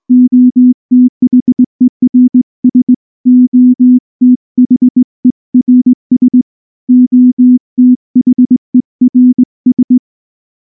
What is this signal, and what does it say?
Morse code, heard as a keyed tone, about 19 wpm, 250 Hz. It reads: OTHERS OTHERS OTHERS
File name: morse-training-20WPM